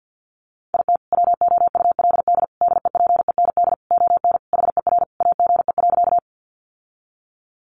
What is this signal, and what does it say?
Morse code, heard as a keyed tone, about 50 wpm, 700 Hz. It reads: IT WOULD DEPEND ON HER AGE.
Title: Sentences from Top 1300 Words 50wpm